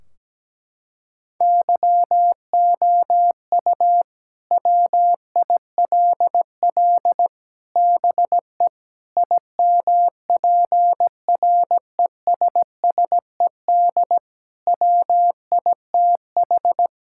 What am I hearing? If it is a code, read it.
Morse code, heard as a keyed tone, about 17 wpm, 700 Hz. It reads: YOU WILL BE IMPRESSED WITH
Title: Sentences from Top 1600 Words 17wpm